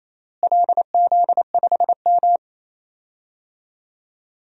Morse code, heard as a keyed tone, about 28 wpm, 700 Hz. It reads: LZ5M
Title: DXCC-100 Call Signs - 4 Characters - Repeated 3x - 28wpm